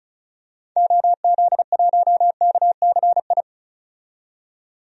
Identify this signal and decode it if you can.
Morse code, heard as a keyed tone, about 36 wpm, 700 Hz. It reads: OZ1KCI
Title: DXCC-100 Call Signs - 6 Characters - 35wpm